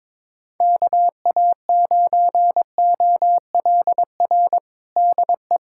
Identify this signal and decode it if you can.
Morse code, heard as a keyed tone, about 22 wpm, 700 Hz. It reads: KA9OLR DE
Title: USA Callsign DE Callsign - 6 Characters - 22wpm